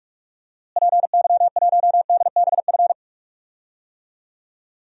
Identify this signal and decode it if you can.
Morse code, heard as a keyed tone, about 45 wpm, 700 Hz. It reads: PY1DBF
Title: DXCC-100 Call Signs - 6 Characters - Repeated 2x - 45wpm